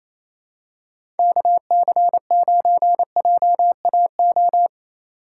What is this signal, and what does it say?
Morse code, heard as a keyed tone, about 28 wpm, 700 Hz. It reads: KC9JAO
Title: USA Call Signs - 6 Characters - 3x Character Spacing - 28wpm